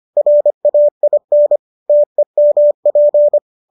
Morse code, heard as a keyed tone, about 25 wpm, 590 Hz. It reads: RAIN TEMP
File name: qso203 25